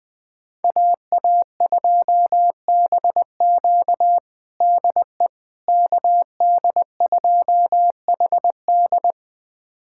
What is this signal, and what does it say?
Morse code, heard as a keyed tone, about 20 wpm, 700 Hz. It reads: AA2BQ DE KD2HD
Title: USA Callsign DE Callsign - 5 Characters - 20wpm